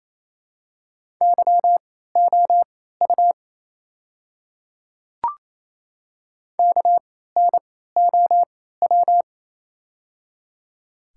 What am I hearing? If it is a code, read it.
Morse code, heard as a keyed tone, about 28 wpm, spaced as for 16 wpm, 700 Hz. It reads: YOU KNOW
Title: 3x Character Spacing - Top 100 Words - 28wpm